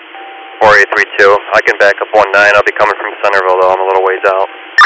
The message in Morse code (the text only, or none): none